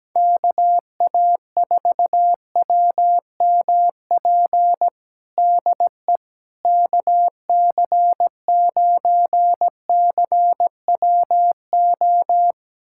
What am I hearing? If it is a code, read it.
Morse code, heard as a keyed tone, about 17 wpm, 700 Hz. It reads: KA4WMP DE KC9CWO
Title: USA Callsign DE Callsign - 6 Characters - 17wpm